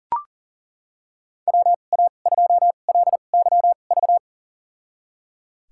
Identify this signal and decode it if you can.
Morse code, heard as a keyed tone, about 37 wpm, spaced as for 28 wpm, 700 Hz. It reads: WA2LYV
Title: USA Call Signs - 6 Characters - 2x Character Spacing - 40wpm